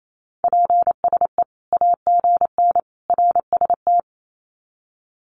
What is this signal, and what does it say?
Morse code, heard as a keyed tone, about 28 wpm, 700 Hz. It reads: PSE AGN RST
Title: Common QSO Phrases 28wpm